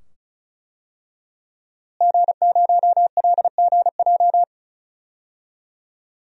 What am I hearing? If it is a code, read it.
Morse code, heard as a keyed tone, about 34 wpm, 700 Hz. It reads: G0LGJ
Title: DXCC-100 Call Signs - 5 Characters - Repeated 2x - 35wpm